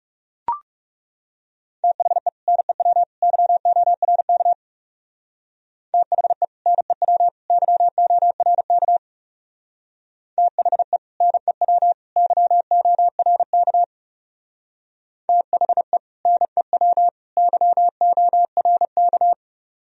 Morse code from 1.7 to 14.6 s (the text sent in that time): THE NEW YORK THE NEW YORK THE NEW YORK